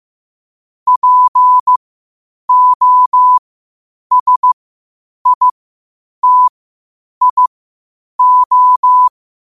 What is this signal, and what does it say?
Morse code, heard as a keyed tone, about 15 wpm, spaced as for 8 wpm, 1000 Hz. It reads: POSITIO
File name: qotdfiles / 8wpm